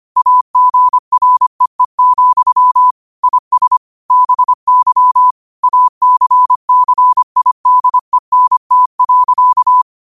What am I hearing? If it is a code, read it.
Morse code, heard as a keyed tone, about 25 wpm, 1000 Hz. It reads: AGREE, IS BY ACCIDENT.
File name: qotdfiles / 25wpm